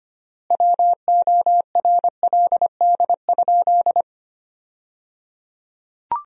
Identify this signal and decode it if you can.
Morse code, heard as a keyed tone, about 25 wpm, 700 Hz. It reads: WORLD?
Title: 3x Word-Spacing - Sentences from Top 200 Words - 25wpm